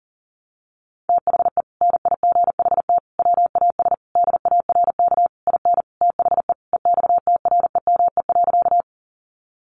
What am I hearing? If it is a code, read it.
Morse code, heard as a keyed tone, about 40 wpm, 700 Hz. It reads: THE NIGHT WAS DARK IN THE EXTREME.
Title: Sentences from Top 1300 Words 40wpm